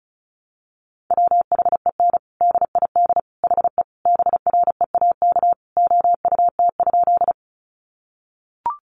Morse code, heard as a keyed tone, about 34 wpm, 700 Hz. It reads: WHEN DID HE BREAK OUT?
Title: Sentences from Top 900 Words 35wpm